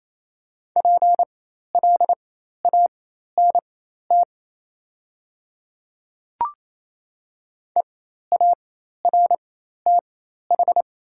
Morse code, heard as a keyed tone, about 27 wpm, spaced as for 13 wpm, 700 Hz. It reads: PLANT EARTH
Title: 4x Character Spacing - Top 300 Words - 28wpm